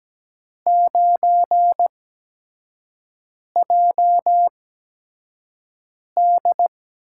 Morse code, heard as a keyed tone, about 17 wpm, spaced as for 5 wpm, 700 Hz. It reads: 9JD
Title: USA Call Signs - 5 Characters - 8x Character Spacing - 17wpm